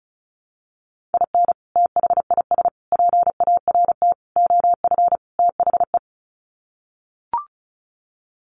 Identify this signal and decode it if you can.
Morse code, heard as a keyed tone, about 34 wpm, 700 Hz. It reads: IN THIS PART OF THE
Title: Sets of 5 Words 35wpm